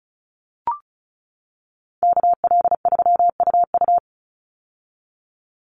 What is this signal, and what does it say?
Morse code, heard as a keyed tone, about 34 wpm, 700 Hz. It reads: KL3UU
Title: USA Call Signs 35wpm